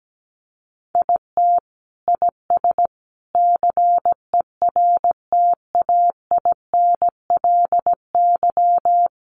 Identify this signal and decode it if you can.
Morse code, heard as a keyed tone, about 17 wpm, 700 Hz. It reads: IT IS CERTAINLY
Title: Sentences from Top 1100 Words 17wpm